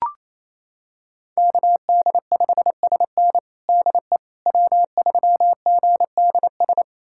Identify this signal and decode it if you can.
Morse code, heard as a keyed tone, about 28 wpm, 700 Hz. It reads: KD5SN DE W3GDS
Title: USA Callsign DE Callsign 28wpm